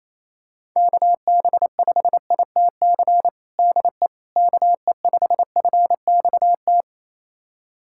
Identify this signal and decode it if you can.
Morse code, heard as a keyed tone, about 28 wpm, 700 Hz. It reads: KB5ITC DE KE5FXT
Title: USA Callsign DE Callsign - 6 Characters - 28wpm